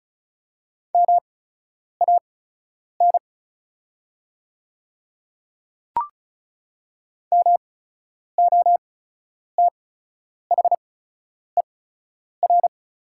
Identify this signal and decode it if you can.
Morse code, heard as a keyed tone, about 36 wpm, spaced as for 10 wpm, 700 Hz. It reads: MAN MOTHER
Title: 8x Character Spacing - Top 200 Words - 35wpm